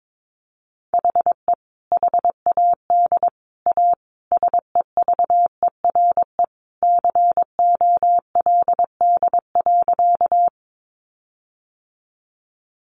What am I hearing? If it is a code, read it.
Morse code, heard as a keyed tone, about 22 wpm, 700 Hz. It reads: HE HAD A SEVERE COLD.
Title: Sentences from Top 1400 Words 22wpm